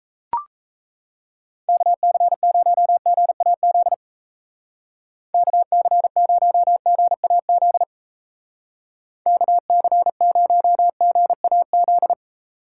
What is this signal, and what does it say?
Morse code, heard as a keyed tone, about 36 wpm, 700 Hz. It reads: KC0GAZ KC0GAZ KC0GAZ